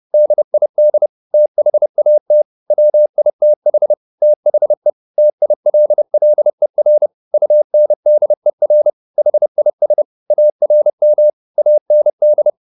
Morse code, heard as a keyed tone, about 30 wpm, 600 Hz. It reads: DID THAT WITH THE TILLER UNDER HIS ARM AND